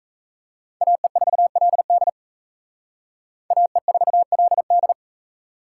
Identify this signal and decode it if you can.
Morse code, heard as a keyed tone, about 36 wpm, 700 Hz. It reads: AE4LD AE4LD